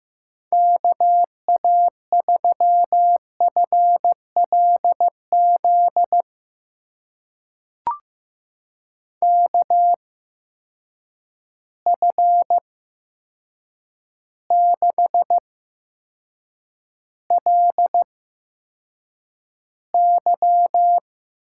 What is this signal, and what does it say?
Morse code, heard as a keyed tone, about 15 wpm, spaced as for 7 wpm, 700 Hz. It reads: KA3FLZ KF6LY
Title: USA Call Signs - 8x Character Spacing - 15wpm